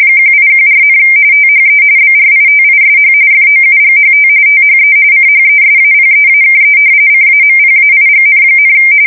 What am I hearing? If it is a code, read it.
RTTY, about 45.45 baud. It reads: 1814, THE BRITISH FOUGHT THE CHESAPEAKE CAMPAIGN,
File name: fsk170 / usnavy2